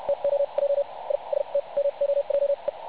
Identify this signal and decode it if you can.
Morse code, heard as a keyed tone, about 71 wpm, spaced as for 42 wpm, 560 Hz, with radio noise. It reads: TO JI ST MOJE
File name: pub / record / websdr recording 3564.8kHz